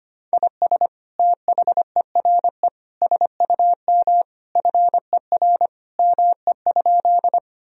Morse code, heard as a keyed tone, about 25 wpm, 700 Hz. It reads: IS THERE SUM FER ME?
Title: Sentences from Top 200 Words and Abbreviations 25wpm